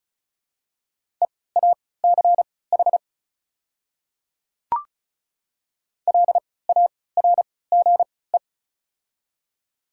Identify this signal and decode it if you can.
Morse code, heard as a keyed tone, about 34 wpm, spaced as for 20 wpm, 700 Hz. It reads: EACH LARGE
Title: 3x Character Spacing - Top 200 Words - 35wpm